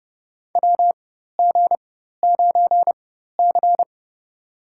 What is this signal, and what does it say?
Morse code, heard as a keyed tone, about 29 wpm, spaced as for 14 wpm, 700 Hz. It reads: WG9C
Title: USA Call Signs - 4 Characters - 4x Character Spacing - 30wpm